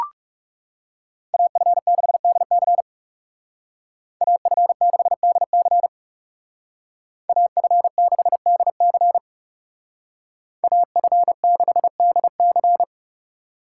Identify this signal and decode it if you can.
Morse code, heard as a keyed tone, about 40 wpm, 700 Hz. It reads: AF6DC AF6DC AF6DC AF6DC